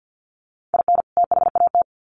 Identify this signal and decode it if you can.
Morse code, heard as a keyed tone, about 50 wpm, 700 Hz. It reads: IN THAT
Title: Sets of 2 Words 50wpm